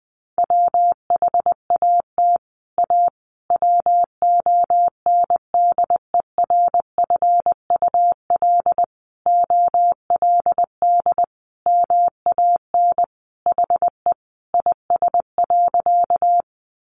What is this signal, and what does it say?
Morse code, heard as a keyed tone, about 20 wpm, 700 Hz. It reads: WHAT A WONDERFUL OLD MAN HE IS.
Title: Sentences from Top 1100 Words 20wpm